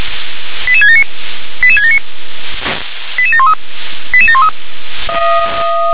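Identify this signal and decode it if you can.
Five-tone selective call (ZVEI), heard as quick runs of five tones, tones of about 80 ms. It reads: 88679 88679 88713 88713
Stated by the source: ZVEI five-tone selective call 88671 received via SDR.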